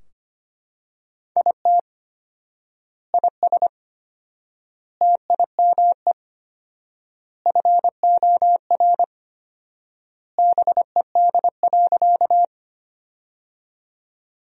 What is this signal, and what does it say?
Morse code, heard as a keyed tone, about 25 wpm, 700 Hz. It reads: IT IS TIME FOR BED.
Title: 4x Word-Spacing - Sentences from Top 500 Words - 25wpm